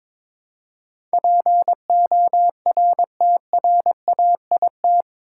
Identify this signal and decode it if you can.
Morse code, heard as a keyed tone, about 22 wpm, 700 Hz. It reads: PORTRAIT
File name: practice / 22wpm / Eight Letter Words - 22wpm